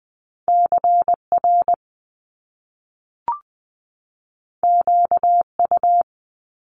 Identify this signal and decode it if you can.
Morse code, heard as a keyed tone, about 20 wpm, 700 Hz. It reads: CR QU
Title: Top Two Letter Combinations in Words 20wpm